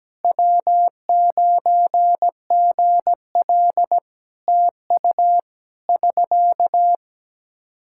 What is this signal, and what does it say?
Morse code, heard as a keyed tone, about 17 wpm, 700 Hz. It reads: W9GL TU <SK>
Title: K1USN SST QSO Practice 17wpm